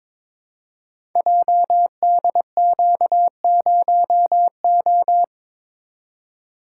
Morse code, heard as a keyed tone, about 23 wpm, 700 Hz. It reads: JDQ0O
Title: Code Groups - 5 Letter:Numbers - 22wpm